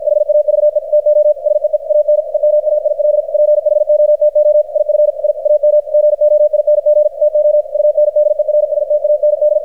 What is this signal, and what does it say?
Morse code, heard as a keyed tone, about 53 wpm, 600 Hz, with radio noise. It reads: HAVE TO SEE AN EMAIL PHOTO IF I AM GOING TO UNDERSTAND
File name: archived / CRSnet